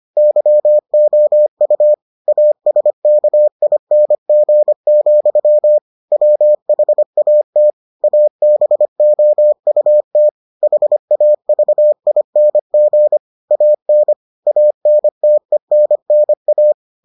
Morse code, heard as a keyed tone, about 25 wpm, 600 Hz. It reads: YOU ASKING, WHAT ABOUT HAVING AN ANTENNA